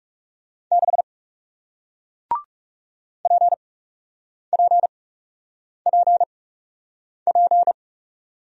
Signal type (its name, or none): Morse code